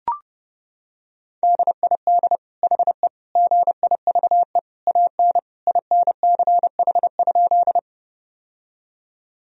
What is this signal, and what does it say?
Morse code, heard as a keyed tone, about 32 wpm, 700 Hz. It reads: DID HE GIVE AN INCH?
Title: Sentences from Top 1000 Words - Review Part 3 - 30wpm